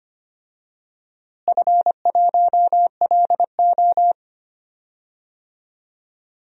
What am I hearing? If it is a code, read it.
Morse code, heard as a keyed tone, about 25 wpm, 700 Hz. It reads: F1LO
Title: DXCC-100 Call Signs - 4 Characters - Repeated 2x - 25wpm